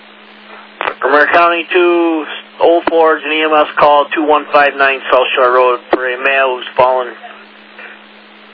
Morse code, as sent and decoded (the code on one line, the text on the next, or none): none